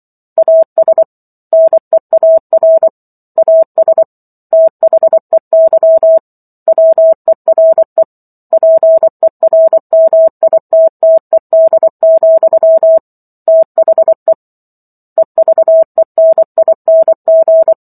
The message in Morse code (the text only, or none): AS NEAR AS THEY WERE PERMITTED, THE EVENING